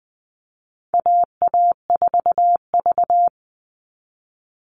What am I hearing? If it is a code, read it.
Morse code, heard as a keyed tone, about 20 wpm, 700 Hz. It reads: AA4V
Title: Call Signs Repeated 2x - 20wpm